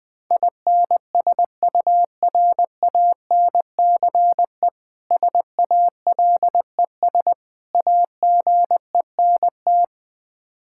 Morse code, heard as a keyed tone, about 20 wpm, 700 Hz. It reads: INSURANCE SALES AGENT
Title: Practice QSO - Exchange 3 - More Info - 20wpm